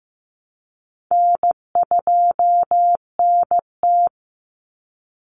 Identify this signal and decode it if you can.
Morse code, heard as a keyed tone, about 15 wpm, 700 Hz. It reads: N2NT
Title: Easy Call Signs Repeated 2x - 15wpm